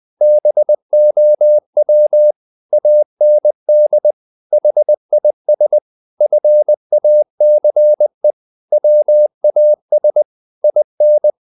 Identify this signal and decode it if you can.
Morse code, heard as a keyed tone, about 20 wpm, 600 Hz. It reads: BOW AND HIS FACE WAS IN
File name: the old man and the sea part 2 20wpm